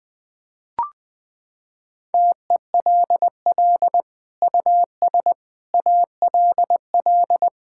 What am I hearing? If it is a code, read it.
Morse code, heard as a keyed tone, about 20 wpm, 700 Hz. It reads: TELL US ALL
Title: Sentences from Top 1000 Words - Review Part 5 - 20wpm